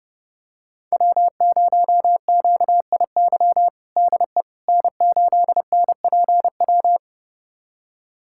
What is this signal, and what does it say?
Morse code, heard as a keyed tone, about 30 wpm, 700 Hz. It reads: W0QIY DE N8NPW
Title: USA Callsign DE Callsign - 5 Characters - 30wpm